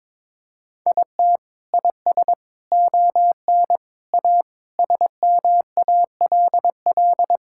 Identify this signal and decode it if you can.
Morse code, heard as a keyed tone, about 22 wpm, 700 Hz. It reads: IT IS ON A SMALL
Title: Sentences from Top 1000 Words - Review Part 5 - 22wpm